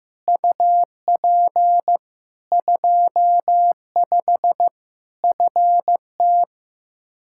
Morse code, heard as a keyed tone, about 16 wpm, 700 Hz. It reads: UP 25 FT
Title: K1USN SST QSO Practice 15wpm